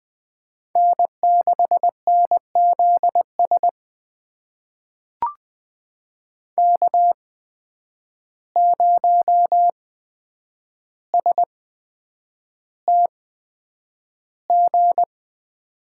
Morse code, heard as a keyed tone, about 20 wpm, spaced as for 9 wpm, 700 Hz. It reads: N6NZS K0STG
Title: USA Call Signs - 5 Characters - 8x Character Spacing - 20wpm